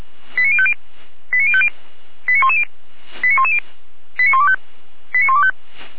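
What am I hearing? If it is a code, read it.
Five-tone selective call (ZVEI), heard as quick runs of five tones, tones of about 70 ms. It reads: 89050 89050 89110 89110 89125 89125
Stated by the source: ZVEI five-tone selective call 89310 received on SDR.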